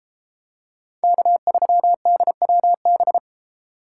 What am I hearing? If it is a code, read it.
Morse code, heard as a keyed tone, about 32 wpm, 700 Hz. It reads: K3DWB